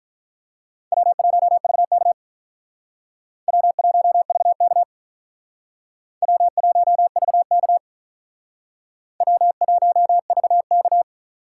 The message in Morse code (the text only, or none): W1VK W1VK W1VK W1VK